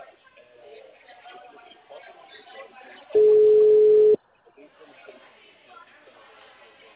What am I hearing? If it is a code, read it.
Ringback tone (Europe, 425 Hz), with speech.
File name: out- -2010- - - .15057